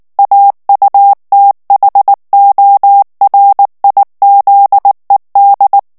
Morse code, heard as a keyed tone, about 19 wpm, 800 Hz. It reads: AUTHORIZED
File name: ANGLAIS texte clair 3 vitesse 18 mots minute